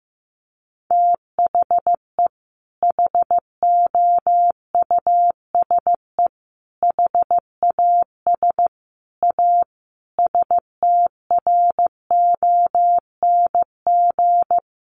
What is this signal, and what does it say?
Morse code, heard as a keyed tone, about 15 wpm, 700 Hz. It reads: THE HOUSE HAS A STRONG
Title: Sentences from Top 1200 Words 15wpm